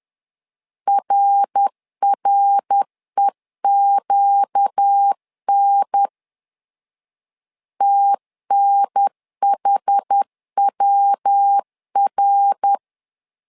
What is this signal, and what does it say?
Morse code, heard as a keyed tone, about 10 wpm, 800 Hz. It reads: RREQN TNHWR
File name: LEZIONI / 4 Q 10wpm 2210